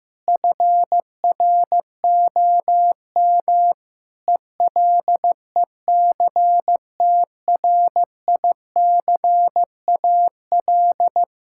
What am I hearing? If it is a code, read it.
Morse code, heard as a keyed tone, about 15 wpm, 700 Hz. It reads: FROM ELECTRICAL
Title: Sentences from Top 1000 Words - Review Part 6 - 15wpm